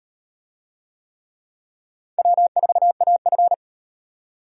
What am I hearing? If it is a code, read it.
Morse code, heard as a keyed tone, about 37 wpm, 700 Hz. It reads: W4AF